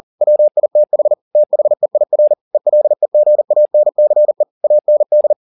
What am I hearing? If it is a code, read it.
Morse code, heard as a keyed tone, about 40 wpm, 600 Hz. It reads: WITH THEIR ELEGANCE AND